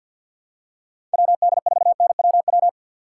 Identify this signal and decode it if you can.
Morse code, heard as a keyed tone, about 49 wpm, 700 Hz. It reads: WD4NWW
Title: USA Call Signs - 6 Characters - 6x Character Spacing - 50wpm